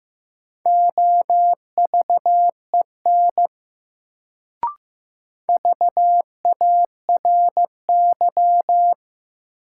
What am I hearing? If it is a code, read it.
Morse code, heard as a keyed tone, about 15 wpm, 700 Hz. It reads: OVEN VARY
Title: Sentences from Top 1200 Words - New Word Drill Alt - 15wpm